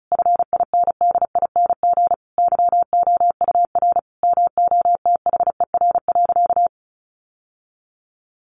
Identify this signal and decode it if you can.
Morse code, heard as a keyed tone, about 34 wpm, 700 Hz. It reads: FINDING YOUR MOTHER.
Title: Sentences from Top 1300 Words 35wpm